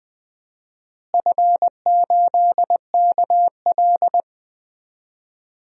Morse code, heard as a keyed tone, about 20 wpm, 700 Hz. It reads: F8KL
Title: DXCC-100 Call Signs - 4 Characters - Repeated 3x - 20wpm